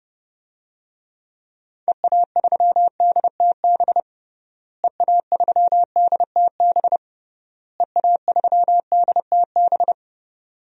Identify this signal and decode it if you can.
Morse code, heard as a keyed tone, about 31 wpm, 700 Hz. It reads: EA3DTB EA3DTB EA3DTB
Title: DXCC-100 Call Signs - 6 Characters - Repeated 3x - 30wpm